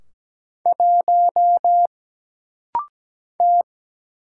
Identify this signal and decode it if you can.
Morse code, heard as a keyed tone, about 17 wpm, 700 Hz. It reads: 1 T
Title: Single Letter-Number - Rapid-Fire 17wpm